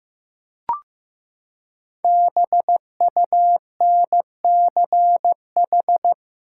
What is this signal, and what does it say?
Morse code, heard as a keyed tone, about 15 wpm, 700 Hz. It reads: BUNCH
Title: Five Letter Words - 15wpm